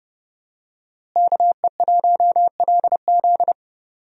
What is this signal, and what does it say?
Morse code, heard as a keyed tone, about 31 wpm, 700 Hz. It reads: KE1LZ
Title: USA Call Signs - 5 Characters - Repeated 2x - 30wpm